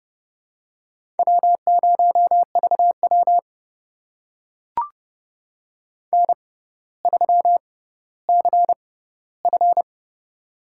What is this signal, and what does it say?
Morse code, heard as a keyed tone, about 30 wpm, spaced as for 15 wpm, 700 Hz. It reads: W0VW N3CF
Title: USA Call Signs - 4 Characters - 6x Character Spacing - 30wpm